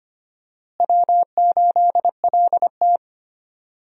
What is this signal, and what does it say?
Morse code, heard as a keyed tone, about 25 wpm, 700 Hz. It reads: W8LT
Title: USA Call Signs - 4 Characters - Repeated 3x - 25wpm